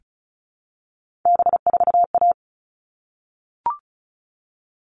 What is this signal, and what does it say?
Morse code, heard as a keyed tone, about 34 wpm, 700 Hz. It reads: B4A
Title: All Cut Numbers - 3-Digit 35wpm